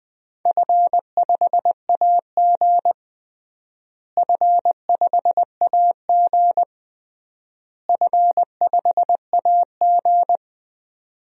Morse code, heard as a keyed tone, about 20 wpm, 700 Hz. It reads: F5AG F5AG F5AG
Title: DXCC-100 Call Signs - 4 Characters - Repeated 3x - 20wpm